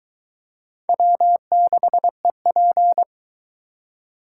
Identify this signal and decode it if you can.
Morse code, heard as a keyed tone, about 22 wpm, 700 Hz. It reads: W6EP